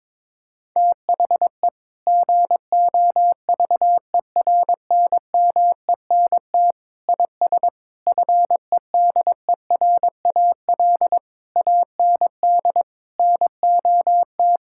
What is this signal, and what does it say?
Morse code, heard as a keyed tone, about 22 wpm, 700 Hz. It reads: THE GOVERNMENT IS FEDERAL AND NOT
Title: Sentences from Top 1000 Words - Review Part 5 - 22wpm